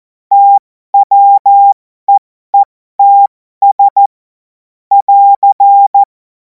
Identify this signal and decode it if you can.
Morse code, heard as a keyed tone, about 14 wpm, spaced as for 12 wpm, 800 Hz. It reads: TWEETS <AR>
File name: Morsekurs / Witze